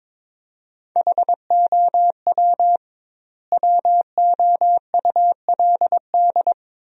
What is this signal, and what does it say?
Morse code, heard as a keyed tone, about 22 wpm, 700 Hz. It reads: HOW WOULD
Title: 2x Word-Spacing - Sentences from Top 100 Words - 22wpm